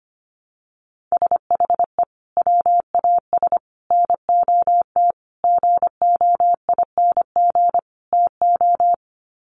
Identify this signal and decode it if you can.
Morse code, heard as a keyed tone, about 25 wpm, 700 Hz. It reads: SHE WAS NOT GOING TO
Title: Sets of 5 Words Encore2 25wpm